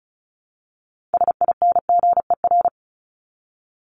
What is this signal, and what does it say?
Morse code, heard as a keyed tone, about 34 wpm, 700 Hz. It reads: SINGER
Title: Top 1501-1750 Words 35wpm